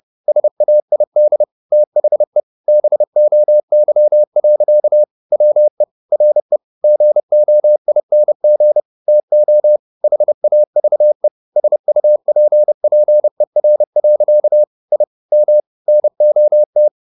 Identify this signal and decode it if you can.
Morse code, heard as a keyed tone, about 30 wpm, 600 Hz. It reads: SAID THE BOY. WE RE GOING TO HAVE SUPPER. I M NOT
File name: the old man and the sea / the old man and the sea 1 30wpm